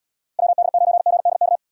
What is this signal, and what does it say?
Morse code, heard as a keyed tone, about 76 wpm, 700 Hz. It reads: WD9GDK